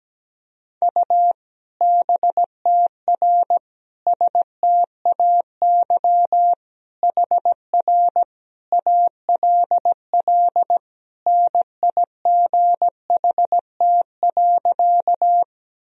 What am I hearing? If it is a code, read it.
Morse code, heard as a keyed tone, about 17 wpm, 700 Hz. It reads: U BTR STAY HR ALL NIGHT.
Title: Sentences from Top 500 Words and Abbreviations 17wpm